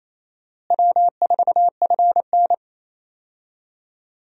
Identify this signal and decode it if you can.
Morse code, heard as a keyed tone, about 28 wpm, 700 Hz. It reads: W4FN
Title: USA Call Signs - 4 Characters - Repeated 2x - 28wpm